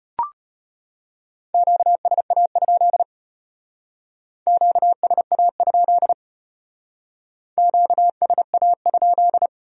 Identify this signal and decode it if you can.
Morse code, heard as a keyed tone, about 33 wpm, 700 Hz. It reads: QSA? QSA? QSA?